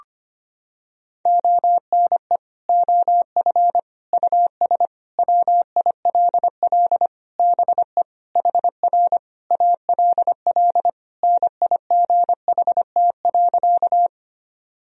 Morse code, heard as a keyed tone, about 25 wpm, 700 Hz. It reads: ONE OF US WILL BE HR ALL NIGHT.
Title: Sentences from Top 300 Words and Abbreviations 25wpm